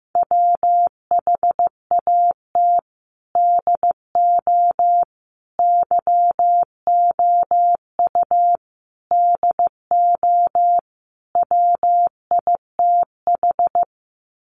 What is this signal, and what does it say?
Morse code, heard as a keyed tone, about 15 wpm, 700 Hz. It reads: WHAT DO YOU DO WITH
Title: Sets of 5 Words Encore2 15wpm